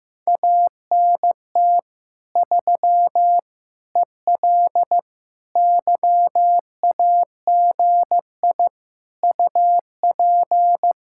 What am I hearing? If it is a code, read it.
Morse code, heard as a keyed tone, about 15 wpm, 700 Hz. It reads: ANT 3 EL YAGI UP
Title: Speed-Racing Common QSO Phrases Encore - 15wpm